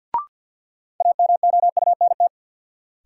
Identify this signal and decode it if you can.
Morse code, heard as a keyed tone, about 53 wpm, 700 Hz. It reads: AMOUNT
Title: Sets of 3 Words - New Word Drill - 50wpm